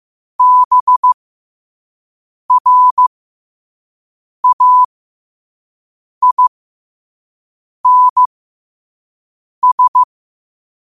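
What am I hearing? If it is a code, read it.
Morse code, heard as a keyed tone, about 15 wpm, spaced as for 5 wpm, 1000 Hz. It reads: BRAINS
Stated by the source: Morse quote of the day at 5 wpm.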